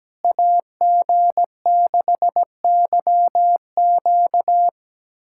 Morse code, heard as a keyed tone, about 17 wpm, 700 Hz. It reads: AG6YQ
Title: USA Callsign DE Callsign 17wpm